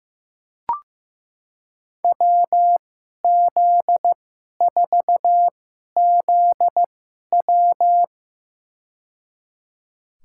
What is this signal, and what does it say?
Morse code, heard as a keyed tone, about 15 wpm, spaced as for 11 wpm, 700 Hz. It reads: WZ4ZW
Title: USA Call Signs - 5 Characters - 2x Character Spacing - 15wpm